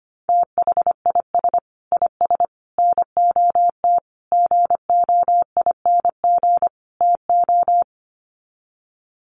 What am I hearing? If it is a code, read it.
Morse code, heard as a keyed tone, about 25 wpm, 700 Hz. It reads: THIS IS NOT GOING TO
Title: Sets of 5 Words Encore2 25wpm